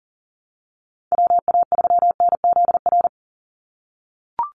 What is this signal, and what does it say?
Morse code, heard as a keyed tone, about 41 wpm, 700 Hz. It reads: WA3NZR
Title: Call Signs - 40wpm